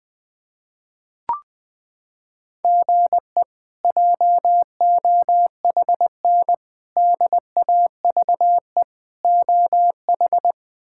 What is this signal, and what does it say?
Morse code, heard as a keyed tone, about 20 wpm, 700 Hz. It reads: GE JOHN DAVE OH
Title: K1USN SST QSO Practice - 20wpm